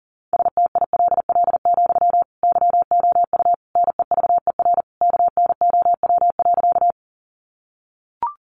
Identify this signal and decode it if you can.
Morse code, heard as a keyed tone, about 37 wpm, 700 Hz. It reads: STILL, YOU NEVER KNOW.
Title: Sentences from Top 300 Words 40wpm